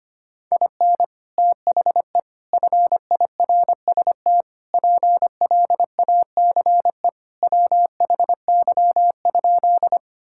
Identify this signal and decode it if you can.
Morse code, heard as a keyed tone, about 25 wpm, 700 Hz. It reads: IN THE FIRST PLACE WHY?
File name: practice / 25wpm / Sentences with 4-Word Phrases - 25wpm